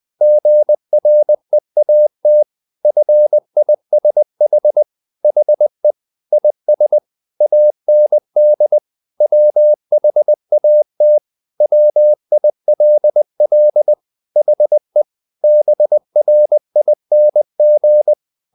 Morse code, heard as a keyed tone, about 20 wpm, 600 Hz. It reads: GREAT FISH HE IS AND WHAT WILL HE BRING
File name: the old man and the sea / the old man and the sea part 1 20wpm